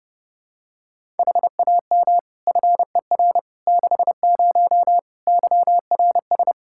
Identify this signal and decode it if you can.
Morse code, heard as a keyed tone, about 32 wpm, 700 Hz. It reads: HAM FER 60 YRS
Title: K1USN SST QSO Practice 30wpm